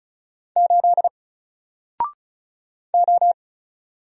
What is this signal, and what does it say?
Morse code, heard as a keyed tone, about 34 wpm, 700 Hz. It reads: 8 O
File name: Single-Letter-Number-35wpm